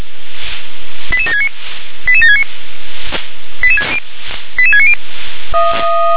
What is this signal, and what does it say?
Five-tone selective call (ZVEI), heard as quick runs of five tones, tones of about 80 ms. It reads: 88769 88769 88690